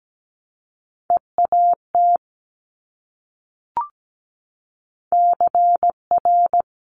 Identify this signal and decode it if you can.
Morse code, heard as a keyed tone, about 17 wpm, 700 Hz. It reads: EAT CR
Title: Multi-letter Phonemes Repeated 3x 17wpm